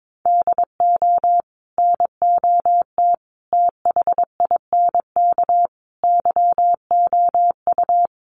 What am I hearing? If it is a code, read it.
Morse code, heard as a keyed tone, about 22 wpm, 700 Hz. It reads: DO NOT THINK YOU
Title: Sets of 4 Words 22wpm